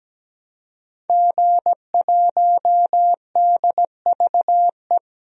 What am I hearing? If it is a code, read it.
Morse code, heard as a keyed tone, about 17 wpm, 700 Hz. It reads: G1DVE
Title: DXCC-100 Call Signs - 5 Characters - 17wpm